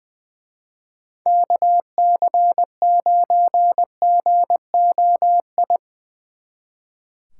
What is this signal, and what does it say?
Morse code, heard as a keyed tone, about 22 wpm, 700 Hz. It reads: KC9GOI
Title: Speed Racing - USA 2x3 Call Signs 20wpm